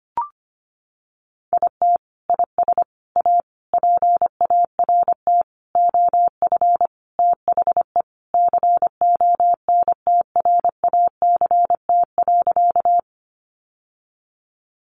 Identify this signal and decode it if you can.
Morse code, heard as a keyed tone, about 25 wpm, 700 Hz. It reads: IT IS A PART OF THE CONTRACT.
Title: Sentences from Top 1400 Words 25wpm